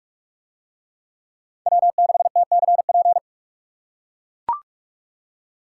Morse code, heard as a keyed tone, about 44 wpm, 700 Hz. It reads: W6TCP